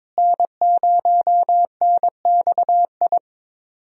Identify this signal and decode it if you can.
Morse code, heard as a keyed tone, about 21 wpm, 700 Hz. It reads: N0NXI
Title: USA Call Signs - 5 Characters - Repeated 3x - 22wpm